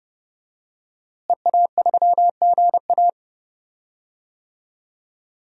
Morse code, heard as a keyed tone, about 29 wpm, 700 Hz. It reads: EA3GA
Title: DXCC-100 Call Signs - 5 Characters - Repeated 3x - 30wpm